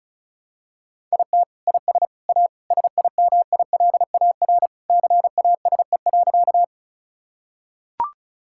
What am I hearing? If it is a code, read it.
Morse code, heard as a keyed tone, about 36 wpm, 700 Hz. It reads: IT IS A SIMILAR CASE.
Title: Sentences from Top 1000 Words - Review Part 5 - 35wpm